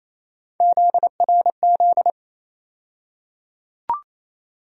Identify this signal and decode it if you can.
Morse code, heard as a keyed tone, about 28 wpm, 700 Hz. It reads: ZRZ
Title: USA Call Sign Postfixes - 3 Letters 28wpm